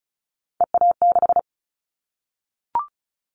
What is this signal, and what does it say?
Morse code, heard as a keyed tone, about 36 wpm, 700 Hz. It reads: EA6
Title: Callsign Prefixes for Top 50 DXCC - 35wpm